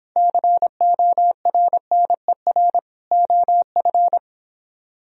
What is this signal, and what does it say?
Morse code, heard as a keyed tone, about 31 wpm, 700 Hz. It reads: CORNER OF